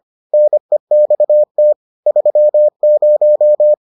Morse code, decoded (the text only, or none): NEXT 30